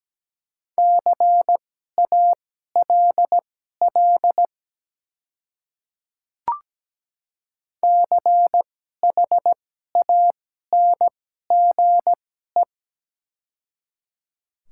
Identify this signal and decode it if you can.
Morse code, heard as a keyed tone, about 17 wpm, spaced as for 12 wpm, 700 Hz. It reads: CALL CHANGE
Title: 2x Character Spacing - Top 200 Words - 17wpm